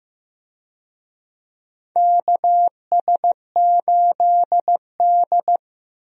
Morse code, heard as a keyed tone, about 15 wpm, 700 Hz. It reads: KS8D
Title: USA Call Signs - 4 Characters - 15wpm